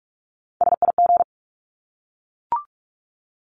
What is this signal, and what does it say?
Morse code, heard as a keyed tone, about 44 wpm, 700 Hz. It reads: SIG
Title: Abbreviations Repeated 3x - 45wpm